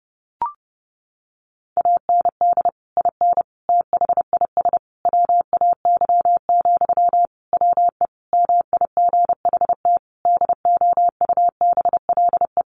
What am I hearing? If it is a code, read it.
Morse code, heard as a keyed tone, about 30 wpm, 700 Hz. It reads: AND IN THIS WAY, WE MIGHT DOUBLE
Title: Sentences from Top 1400 Words 30wpm